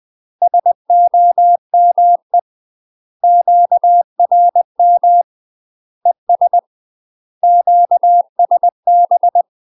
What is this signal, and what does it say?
Morse code, heard as a keyed tone, about 20 wpm, 700 Hz. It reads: SOME QRM ES QSB
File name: UK MASTER QSOs TRAINING / 20WPM / UK MASTER QSOs 0008